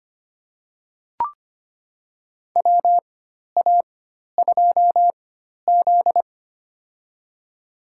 Morse code, heard as a keyed tone, about 25 wpm, spaced as for 12 wpm, 700 Hz. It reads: WA2Z
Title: USA Call Signs - 4x Character Spacing - 25wpm